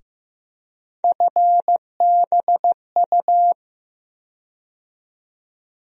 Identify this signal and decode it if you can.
Morse code, heard as a keyed tone, about 15 wpm, 700 Hz. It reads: FBU
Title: USA Call Sign Postfixes - 3 Letters 15wpm